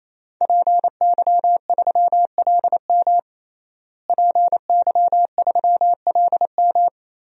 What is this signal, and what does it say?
Morse code, heard as a keyed tone, about 28 wpm, 700 Hz. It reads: PY3LM PY3LM
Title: DXCC-100 Call Signs - 5 Characters - Repeated 3x - 28wpm